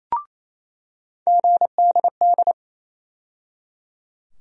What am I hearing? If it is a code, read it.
Morse code, heard as a keyed tone, about 27 wpm, 700 Hz. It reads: GDD
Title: USA Call Sign Postfixes - 3 Letters 28wpm